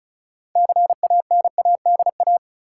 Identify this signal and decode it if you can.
Morse code, heard as a keyed tone, about 34 wpm, 700 Hz. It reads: CANADA
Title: Sets of 5 Words - New Word Drill - 35wpm